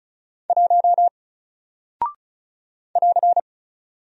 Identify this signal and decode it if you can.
Morse code, heard as a keyed tone, about 36 wpm, 700 Hz. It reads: 1 <AR>
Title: Single Character (Letter:Number:Punctuation:Prosign) Alt 35wpm